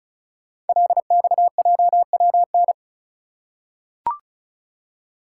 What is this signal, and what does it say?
Morse code, heard as a keyed tone, about 36 wpm, 700 Hz. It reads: LXJWN
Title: Code Groups - 5 Letter:Numbers - 35wpm